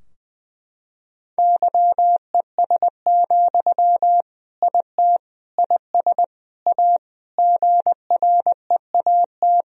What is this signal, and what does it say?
Morse code, heard as a keyed tone, about 19 wpm, 700 Hz. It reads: YES, IT IS A GREAT
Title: Sentences from Top 1000 Words - Review Part 2 - 20wpm